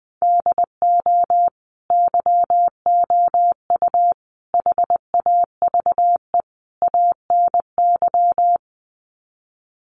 Morse code, heard as a keyed tone, about 20 wpm, 700 Hz. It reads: DO YOU HAVE ANY
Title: Sets of 4 Words 20wpm